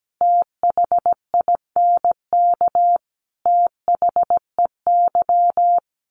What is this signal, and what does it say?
Morse code, heard as a keyed tone, about 17 wpm, 700 Hz. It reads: THINK THEY
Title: Sets of 3 Words 17wpm